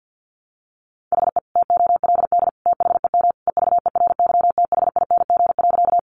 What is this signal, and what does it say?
Morse code, heard as a keyed tone, about 50 wpm, 700 Hz. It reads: HE TOLD THEM EVERYTHING.
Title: Sentences from Top 700 Words 50wpm